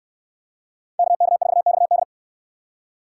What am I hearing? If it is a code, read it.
Morse code, heard as a keyed tone, about 64 wpm, 700 Hz. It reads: KK4XD